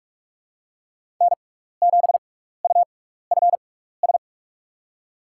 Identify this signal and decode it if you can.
Morse code, heard as a keyed tone, about 48 wpm, spaced as for 16 wpm, 700 Hz. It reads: N7UFS
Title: USA Call Signs - 5 Characters - 6x Character Spacing - 45wpm